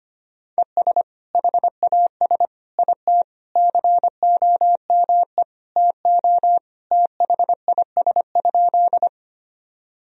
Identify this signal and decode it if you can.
Morse code, heard as a keyed tone, about 25 wpm, 700 Hz. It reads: ES HAS IT COME TO THIS?
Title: Sentences from Top 100 Words and Abbreviations 25wpm